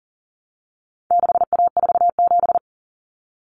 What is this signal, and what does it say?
Morse code, heard as a keyed tone, about 40 wpm, 700 Hz. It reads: 6A47
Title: Common Cut Numbers - 4-Digit 40wpm